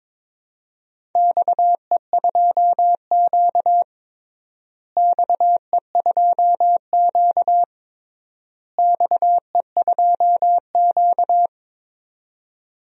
Morse code, heard as a keyed tone, about 22 wpm, 700 Hz. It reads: XE2Q XE2Q XE2Q
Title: DXCC-100 Call Signs - 4 Characters - Repeated 3x - 22wpm